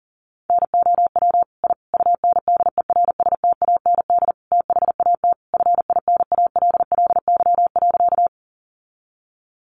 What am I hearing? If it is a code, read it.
Morse code, heard as a keyed tone, about 40 wpm, 700 Hz. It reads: NOW I UNDERSTAND THAT FINALLY.
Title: Sentences from Top 1000 Words 40wpm